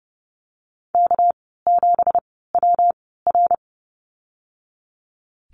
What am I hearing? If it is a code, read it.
Morse code, heard as a keyed tone, about 30 wpm, spaced as for 17 wpm, 700 Hz. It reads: K7WR
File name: Lesson 37g - Letter x - Call Signs 30wpm